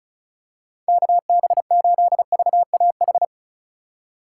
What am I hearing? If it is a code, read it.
Morse code, heard as a keyed tone, about 34 wpm, 700 Hz. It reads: KB8VAH
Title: USA Call Signs - 6 Characters - 3x Character Spacing - 35wpm